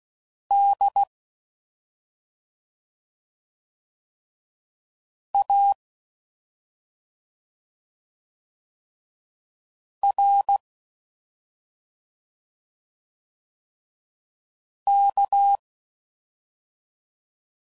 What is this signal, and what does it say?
Morse code, heard as a keyed tone, about 16 wpm, 800 Hz. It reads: D A R K